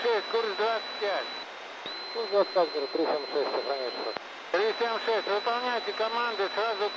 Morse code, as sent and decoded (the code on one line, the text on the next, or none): none